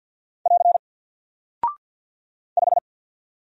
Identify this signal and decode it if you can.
Morse code, heard as a keyed tone, about 48 wpm, 700 Hz. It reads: <AR> 5
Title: Single Character (Letter:Number:Punctuation:Prosign) Alt 50wpm